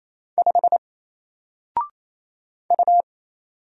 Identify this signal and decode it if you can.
Morse code, heard as a keyed tone, about 27 wpm, 700 Hz. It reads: 5 U
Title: Single Letter-Number Alt 28wpm